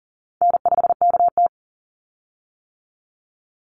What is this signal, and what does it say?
Morse code, heard as a keyed tone, about 40 wpm, 700 Hz. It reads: N5KT
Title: Easy Call Signs - 40wpm